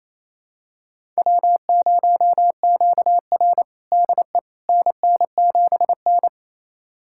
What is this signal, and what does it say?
Morse code, heard as a keyed tone, about 27 wpm, 700 Hz. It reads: W0QR DE NN7N
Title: USA Callsign DE Callsign - 4 Characters - 28wpm